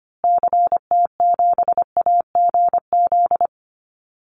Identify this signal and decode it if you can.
Morse code, heard as a keyed tone, about 25 wpm, 700 Hz. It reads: CT7AGZ
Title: Call Signs Repeated 2x - 25wpm